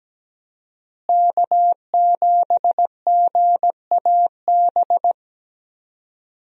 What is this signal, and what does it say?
Morse code, heard as a keyed tone, about 17 wpm, 700 Hz. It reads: K7GAB
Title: USA Call Signs - 5 Characters - Repeated 2x - 17wpm